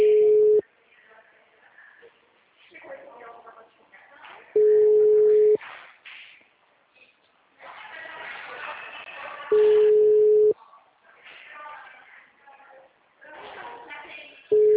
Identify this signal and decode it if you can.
Ringback tone (Europe, 425 Hz), with speech.
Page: Dracena (Call-Center)